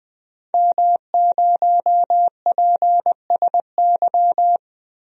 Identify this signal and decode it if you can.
Morse code, heard as a keyed tone, about 21 wpm, 700 Hz. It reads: M0PSY
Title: DXCC-100 Call Signs - 5 Characters - Repeated 2x - 20wpm